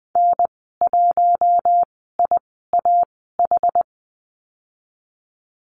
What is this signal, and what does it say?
Morse code, heard as a keyed tone, about 20 wpm, spaced as for 15 wpm, 700 Hz. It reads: N1IAH
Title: 062 Lesson 11g - Letter h - Call Signs 20wpm